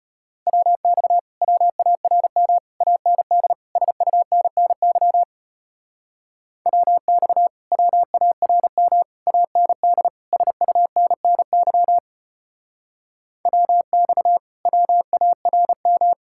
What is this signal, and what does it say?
Morse code, heard as a keyed tone, about 33 wpm, 700 Hz. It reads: WX WARM AND SUNNY WX WARM AND SUNNY WX WARM